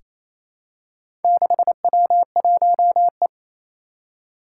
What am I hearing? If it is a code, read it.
Morse code, heard as a keyed tone, about 28 wpm, 700 Hz. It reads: 6W1E
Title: Four Challenging Letter-Number Combinations 28wpm